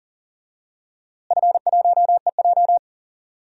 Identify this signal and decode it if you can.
Morse code, heard as a keyed tone, about 40 wpm, 700 Hz. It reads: F1EJ
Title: DXCC-100 Call Signs - 4 Characters - Repeated 2x - 40wpm